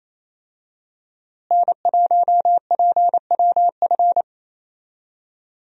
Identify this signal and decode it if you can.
Morse code, heard as a keyed tone, about 28 wpm, 700 Hz. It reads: N1PWF
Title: USA Call Signs - 5 Characters - Repeated 3x - 28wpm